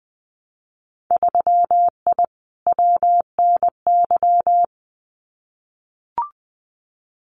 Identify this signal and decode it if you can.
Morse code, heard as a keyed tone, about 20 wpm, 700 Hz. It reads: 3I WNY
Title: Winter Field Day Exchanges 20wpm